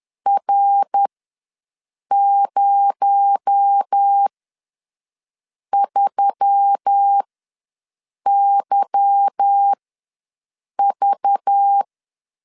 Morse code, heard as a keyed tone, about 10 wpm, spaced as for 6 wpm, 800 Hz. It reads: R03YV
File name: LEZIONI / 8 YJ45 / 8 YJ ws 10wpm 4043